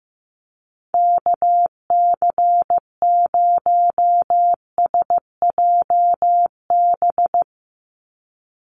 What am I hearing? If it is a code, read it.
Morse code, heard as a keyed tone, about 15 wpm, 700 Hz. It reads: KC0SJB
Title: USA Call Signs 15wpm